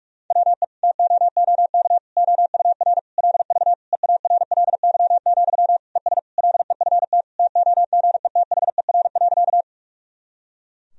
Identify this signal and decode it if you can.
Morse code, heard as a keyed tone, about 46 wpm, 700 Hz. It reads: WE TOOK OUR LV EARLY, ES LEFT TOGETHER.
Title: Sentences from Top 400 Words and Abbreviations 45wpm